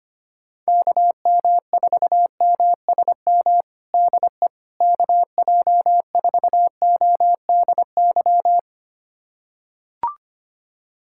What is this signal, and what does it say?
Morse code, heard as a keyed tone, about 25 wpm, 700 Hz. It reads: KM4MSM DE KJ4ODY
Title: USA Callsign DE Callsign 25wpm